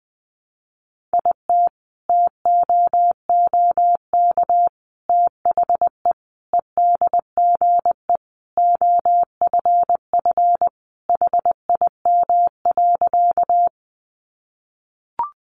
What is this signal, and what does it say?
Morse code, heard as a keyed tone, about 20 wpm, 700 Hz. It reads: IT TOOK THE EDGE OFF HIM.
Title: Sentences from Top 1300 Words 20wpm